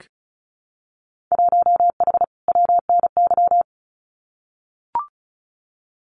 Morse code, heard as a keyed tone, about 36 wpm, 700 Hz. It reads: 1H WNY
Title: Winter Field Day Exchanges 35wpm